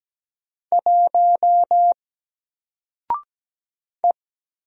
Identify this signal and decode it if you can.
Morse code, heard as a keyed tone, about 17 wpm, 700 Hz. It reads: 1 E
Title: Single Character (Letter:Number:Punctuation:Prosign) Alt 17wpm